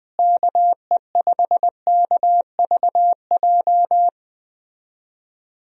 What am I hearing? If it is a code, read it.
Morse code, heard as a keyed tone, about 20 wpm, 700 Hz. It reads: KE5KVJ
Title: USA Call Signs - 6 Characters - Repeated 3x - 20wpm